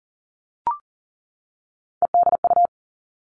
Morse code, heard as a keyed tone, about 40 wpm, 700 Hz. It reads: EDU
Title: All Cut Numbers - 3-Digit 40wpm